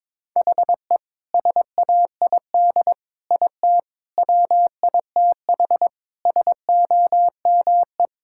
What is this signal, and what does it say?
Morse code, heard as a keyed tone, about 22 wpm, 700 Hz. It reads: HE SAID IT WITH SOME
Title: Sentences from Top 1600 Words 22wpm